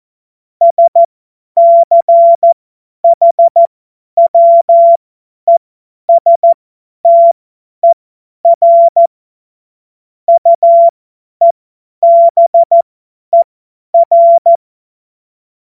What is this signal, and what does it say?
Morse code, heard as a keyed tone, about 14 wpm, spaced as for 10 wpm, 680 Hz. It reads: SCHWESTER UEBER
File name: Morsekurs / Witze / Witz 11. Juli 2023 50 F70